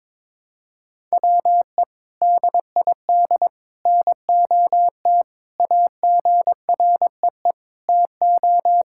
Morse code, heard as a keyed tone, about 22 wpm, 700 Hz. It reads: WE DID NOT AGREE TO
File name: practice / 22wpm / Sentences with 3-Word Phrases - 22wpm